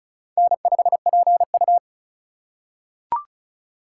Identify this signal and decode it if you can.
Morse code, heard as a keyed tone, about 36 wpm, 700 Hz. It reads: N5PU
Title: K1USN SST QSO Practice - 35wpm